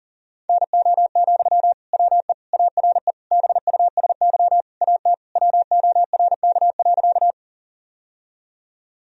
Morse code, heard as a keyed tone, about 37 wpm, 700 Hz. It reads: NO, WE ARE BUSY AT WORK.
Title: Sentences from Top 1000 Words - Review Part 3 - 40wpm